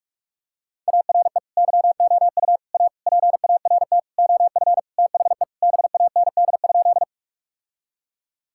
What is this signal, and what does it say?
Morse code, heard as a keyed tone, about 46 wpm, 700 Hz. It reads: ARE YOU A PART OF THE BAND?
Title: Sentences from Top 1600 Words 45wpm